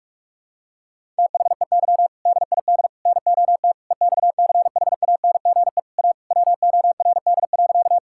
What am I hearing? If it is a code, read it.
Morse code, heard as a keyed tone, about 46 wpm, 700 Hz. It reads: THEY DID NOT EXCHANGE A WORD.
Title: Sentences from Top 1000 Words - Review Part 6 - 45wpm